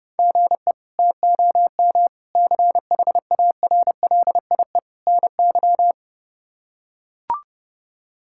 Morse code, heard as a keyed tone, about 30 wpm, 700 Hz. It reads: GE TOM CHARLIE NY
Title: K1USN SST QSO Practice - 30wpm